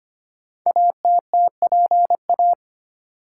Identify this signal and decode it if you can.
Morse code, heard as a keyed tone, about 25 wpm, 700 Hz. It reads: ATTPA
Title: Code Groups - 5 Letters - 25wpm